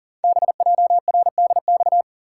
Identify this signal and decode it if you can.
Morse code, heard as a keyed tone, about 40 wpm, 700 Hz. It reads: BJRDX
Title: Code Groups - 5 Letters - 40wpm